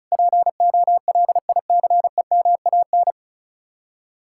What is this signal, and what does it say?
Morse code, heard as a keyed tone, about 46 wpm, 700 Hz. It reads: POLICEMAN